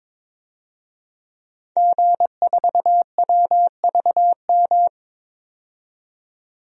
Morse code, heard as a keyed tone, about 22 wpm, 700 Hz. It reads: G4WVM
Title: DXCC-100 Call Signs - 5 Characters - 22wpm